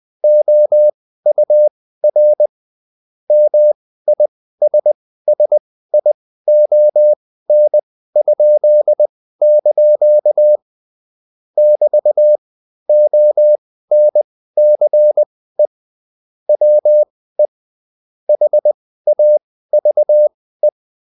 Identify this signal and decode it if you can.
Morse code, heard as a keyed tone, about 20 wpm, spaced as for 14 wpm, 600 Hz. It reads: OUR MISSION?) <BT> ONCE WE HAVE